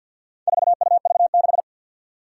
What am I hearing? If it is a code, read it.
Morse code, heard as a keyed tone, about 51 wpm, 700 Hz. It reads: 4UV6
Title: Four Challenging Letter-Number Combinations 50wpm